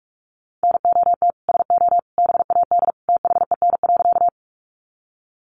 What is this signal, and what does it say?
Morse code, heard as a keyed tone, about 44 wpm, 700 Hz. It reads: NOT SO BAD THEN.
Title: Sentences from Top 700 Words 45wpm